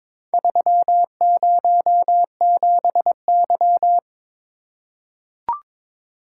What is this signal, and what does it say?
Morse code, heard as a keyed tone, about 22 wpm, 700 Hz. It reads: 307Y
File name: Four Letter-Number Combinations - 22wpm